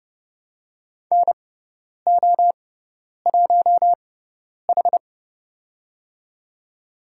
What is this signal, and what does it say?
Morse code, heard as a keyed tone, about 30 wpm, spaced as for 10 wpm, 700 Hz. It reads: NO1H
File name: Lesson 16g - Letter c - Call Signs 30wpm